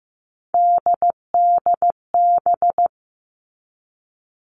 Morse code, heard as a keyed tone, about 15 wpm, 700 Hz. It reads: DDB
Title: All Cut Numbers - 3-Digit 15wpm